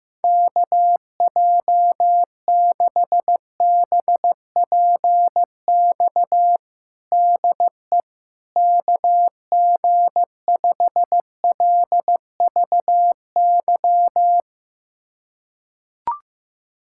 Morse code, heard as a keyed tone, about 15 wpm, 700 Hz. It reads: KJ6BPX DE KG5LVY
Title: USA Callsign DE Callsign - 6 Characters - 15wpm